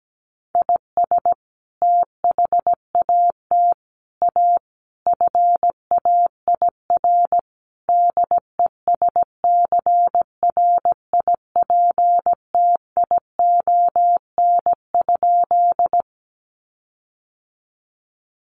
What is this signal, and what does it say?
Morse code, heard as a keyed tone, about 17 wpm, 700 Hz. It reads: IS THAT A FAIR DESCRIPTION?
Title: Sentences from Top 1300 Words 17wpm